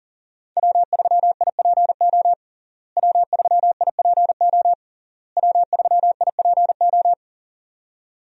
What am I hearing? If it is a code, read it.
Morse code, heard as a keyed tone, about 40 wpm, 700 Hz. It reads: W3IPO W3IPO W3IPO
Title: USA Call Signs - 5 Characters - Repeated 3x - 40wpm